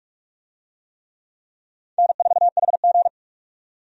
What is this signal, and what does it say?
Morse code, heard as a keyed tone, about 46 wpm, 700 Hz. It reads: N4HG
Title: USA Call Signs - 4 Characters - 45wpm